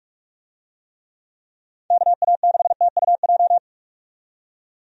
Morse code, heard as a keyed tone, about 47 wpm, 700 Hz. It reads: KA6TUJ
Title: USA Call Signs - 6 Characters - 45wpm